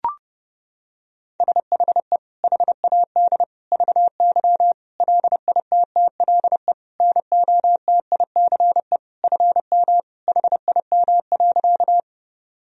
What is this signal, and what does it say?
Morse code, heard as a keyed tone, about 32 wpm, 700 Hz. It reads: SHE HAD VY LITTLE NOTICE FM HIM.
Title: Sentences from Top 400 Words and Abbreviations 30wpm